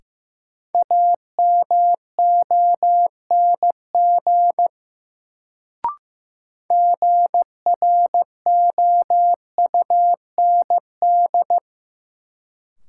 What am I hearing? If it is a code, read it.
Morse code, heard as a keyed tone, about 15 wpm, 700 Hz. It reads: AMONG GROUND
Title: Sentences from Top 500 Words - New Word Drill - 15wpm